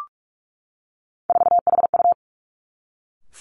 Morse code, heard as a keyed tone, about 46 wpm, 700 Hz. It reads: 4HU
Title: Three Challenging Letter-Number Combinations 45wpm